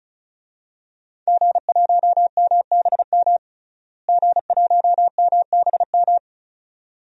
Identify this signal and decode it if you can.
Morse code, heard as a keyed tone, about 34 wpm, 700 Hz. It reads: G1MBM G1MBM
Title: DXCC-100 Call Signs - 5 Characters - Repeated 2x - 35wpm